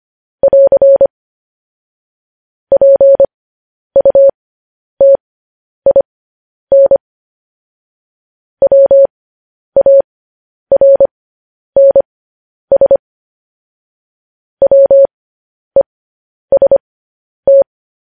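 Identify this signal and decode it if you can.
Morse code, heard as a keyed tone, about 25 wpm, spaced as for 10 wpm, 550 Hz. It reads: <AR> PUTIN WARNS WEST
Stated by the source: Morse news headlines at 10.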